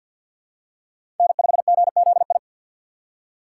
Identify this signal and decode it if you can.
Morse code, heard as a keyed tone, about 46 wpm, 700 Hz. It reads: N5GZI
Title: USA Call Signs - 5 Characters - 6x Character Spacing - 50wpm